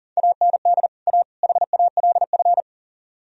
Morse code, heard as a keyed tone, about 53 wpm, 700 Hz. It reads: AND A HALF